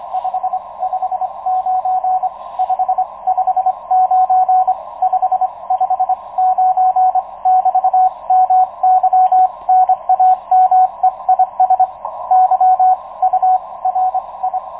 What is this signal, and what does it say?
Morse code, heard as a keyed tone, about 25 wpm, spaced as for 21 wpm, 740 Hz, with radio noise. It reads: 559 559 559 <BT> MY NAME IS YURI
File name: Expedition 2019 / ua1ceg-p 10.22.2019 09.36 utc audio DL1HSI